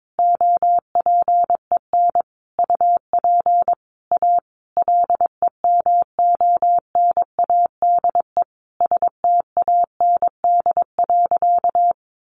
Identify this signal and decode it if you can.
Morse code, heard as a keyed tone, about 22 wpm, 700 Hz. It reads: OPEN UP A LEMONADE STAND.
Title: Additional Easy Phrases 22wpm